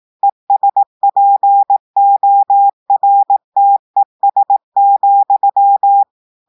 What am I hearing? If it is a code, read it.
Morse code, heard as a keyed tone, about 18 wpm, 800 Hz. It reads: ESPORTES,
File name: audios / portugais / portugais texte en clair 2 vitesse 18 mots minutes